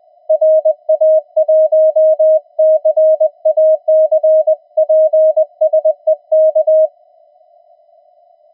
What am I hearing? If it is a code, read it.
Morse code, heard as a keyed tone, about 20 wpm, 630 Hz. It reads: RA1CAC PSE K
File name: attachments / TX 7030